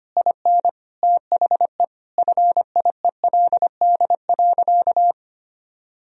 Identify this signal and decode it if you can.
Morse code, heard as a keyed tone, about 25 wpm, 700 Hz. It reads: IN THE FIELD.
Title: Sentences from Top 1000 Words - Review Part 2 - 25wpm